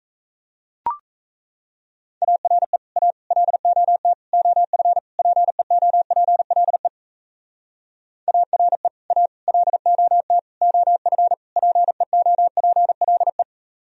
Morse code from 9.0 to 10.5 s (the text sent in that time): A LOT